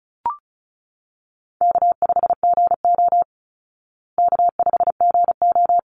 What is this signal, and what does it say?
Morse code, heard as a keyed tone, about 34 wpm, 700 Hz. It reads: K5GO K5GO
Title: Easy Call Signs Repeated 2x - 35wpm